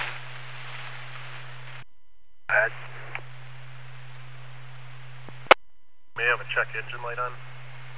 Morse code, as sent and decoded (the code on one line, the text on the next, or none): none